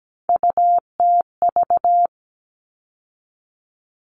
Morse code, heard as a keyed tone, about 17 wpm, 700 Hz. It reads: UTV
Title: All Cut Numbers - 3-Digit 17wpm